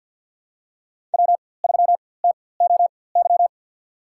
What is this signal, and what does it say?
Morse code, heard as a keyed tone, about 47 wpm, spaced as for 23 wpm, 700 Hz. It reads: W3TOY
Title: USA Call Signs - 5 Characters - 4x Character Spacing - 50wpm